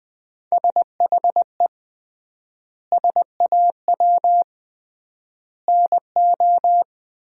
Morse code, heard as a keyed tone, about 21 wpm, 700 Hz. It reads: SHE SAW NO
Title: 3x Word-Spacing - Sentences from Top 300 Words - 20wpm